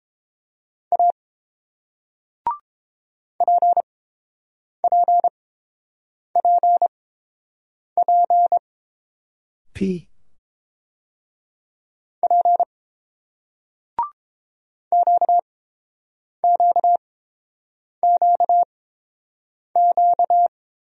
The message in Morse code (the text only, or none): A PPPPP QQQQ